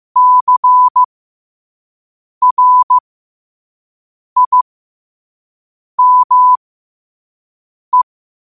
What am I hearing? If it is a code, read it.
Morse code, heard as a keyed tone, about 15 wpm, spaced as for 5 wpm, 1000 Hz. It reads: CRIME
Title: 5 WPM morse code quotes for Fri, 01 Mar 2024